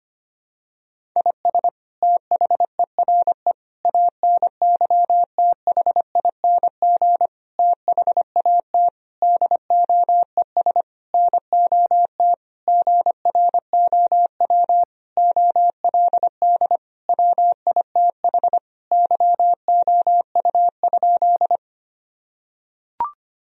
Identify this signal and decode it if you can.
Morse code, heard as a keyed tone, about 25 wpm, 700 Hz. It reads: IS THERE ANYTHING THAT DOES NOT GROW OLD WITH YOU?
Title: Sentences from Top 1000 Words - Review Part 1 - 25wpm